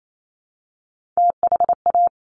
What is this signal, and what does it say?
Morse code, heard as a keyed tone, about 28 wpm, 690 Hz. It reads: THA
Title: Top Three Letter Combinations in Words 28wpm